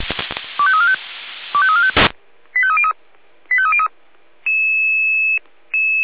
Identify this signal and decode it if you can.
Five-tone selective call (ZVEI), heard as quick runs of five tones, tones of about 70 ms. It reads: 25345 25345 85303 85303
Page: SDR/ZVEI